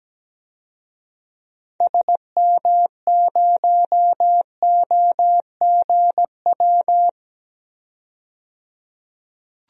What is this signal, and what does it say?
Morse code, heard as a keyed tone, about 17 wpm, 700 Hz. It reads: SM0OGW
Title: DXCC-100 Call Signs - 6 Characters - 17wpm